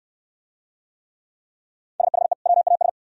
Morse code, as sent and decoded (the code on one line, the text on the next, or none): ... .... / .-- .- ...
SHE WAS